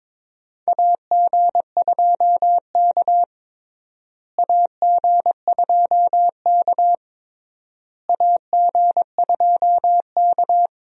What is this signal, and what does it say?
Morse code, heard as a keyed tone, about 22 wpm, 700 Hz. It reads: AG2K AG2K AG2K
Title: USA Call Signs - 4 Characters - Repeated 3x - 22wpm